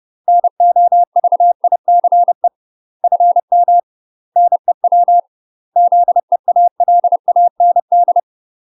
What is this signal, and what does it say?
Morse code, heard as a keyed tone, about 30 wpm, 700 Hz. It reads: NOVICE FM NEW ZEALAND
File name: UK MASTER QSOs TRAINING / 30WPM / UK MASTER QSOs 0025